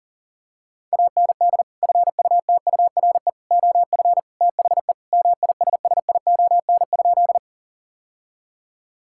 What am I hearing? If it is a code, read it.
Morse code, heard as a keyed tone, about 40 wpm, 700 Hz. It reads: AND FUTURE OF THE MISSION?
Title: Sentences from Top 1600 Words 40wpm